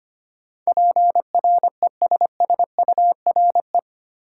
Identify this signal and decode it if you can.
Morse code, heard as a keyed tone, about 25 wpm, 700 Hz. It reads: PRESSURE
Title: Sentences from Top 1000 Words - Review Part 6 - New Word Drill - 25wpm